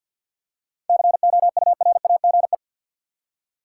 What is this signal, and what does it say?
Morse code, heard as a keyed tone, about 52 wpm, 700 Hz. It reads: COURAGE
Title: Seven Letter Words - 50wpm